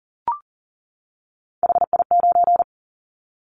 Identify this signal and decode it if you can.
Morse code, heard as a keyed tone, about 38 wpm, 700 Hz. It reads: HI9
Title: Callsign Prefixes for Top 100 DXCC - 40wpm